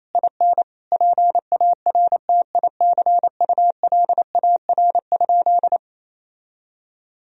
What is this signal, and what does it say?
Morse code, heard as a keyed tone, about 27 wpm, 700 Hz. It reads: IN PARTICULAR?
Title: Sentences from Top 1000 Words - Review Part 4 - 28wpm